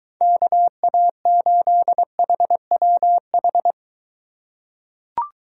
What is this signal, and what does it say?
Morse code, heard as a keyed tone, about 23 wpm, 700 Hz. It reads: KA8HWH